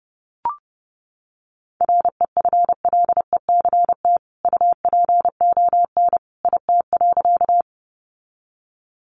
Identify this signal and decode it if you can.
Morse code, heard as a keyed tone, about 30 wpm, 700 Hz. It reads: REFLECT UPON IT.
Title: Sentences from Top 1500 Words 30wpm